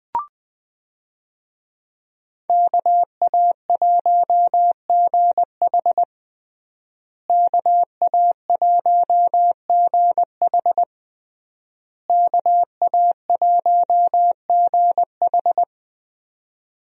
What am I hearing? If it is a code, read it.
Morse code, heard as a keyed tone, about 20 wpm, 700 Hz. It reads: KA1GH KA1GH KA1GH
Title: USA Call Signs - 5 Characters - Repeated 3x - 20wpm